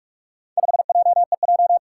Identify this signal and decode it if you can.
Morse code, heard as a keyed tone, about 42 wpm, 700 Hz. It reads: 5JEJ
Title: Four Challenging Letter-Number Combinations 45wpm